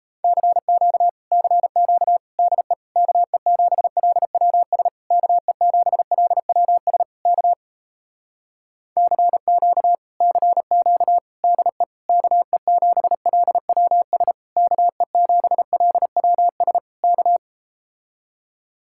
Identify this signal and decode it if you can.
Morse code, heard as a keyed tone, about 37 wpm, 700 Hz. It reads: CQ CQ DE KE7LWS KE7LWS K CQ CQ DE KE7LWS KE7LWS K